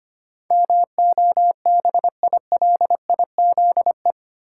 Morse code, heard as a keyed tone, about 25 wpm, 700 Hz. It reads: MOBILIZE
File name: Eight Letter Words - 25wpm